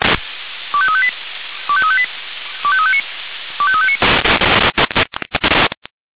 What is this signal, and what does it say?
Five-tone selective call (ZVEI), heard as quick runs of five tones, tones of about 70 ms. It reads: 25358 25358 25359 25359
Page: SDR/ZVEI/25310